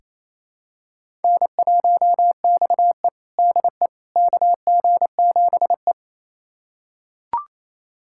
Morse code, heard as a keyed tone, about 28 wpm, 700 Hz. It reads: N1XE DE KG7E
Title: USA Callsign DE Callsign - 4 Characters - 28wpm